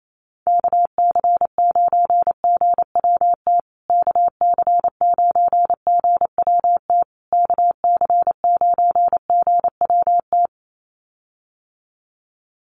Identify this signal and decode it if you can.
Morse code, heard as a keyed tone, about 28 wpm, 700 Hz. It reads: KC9GWT KC9GWT KC9GWT
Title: USA Call Signs Repeated 3x 28wpm